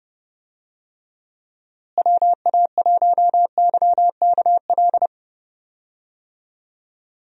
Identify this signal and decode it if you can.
Morse code, heard as a keyed tone, about 31 wpm, 700 Hz. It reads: WA1YKL
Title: USA Call Signs - 6 Characters - Repeated 2x - 30wpm